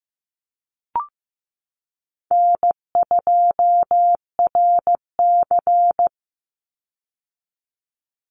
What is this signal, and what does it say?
Morse code, heard as a keyed tone, about 15 wpm, 700 Hz. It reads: N2RC
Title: Easy Call Signs Repeated 2x - 15wpm